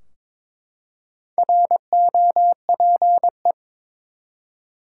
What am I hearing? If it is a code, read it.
Morse code, heard as a keyed tone, about 22 wpm, 700 Hz. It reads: ROPE
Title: Four Letter Words - 22wpm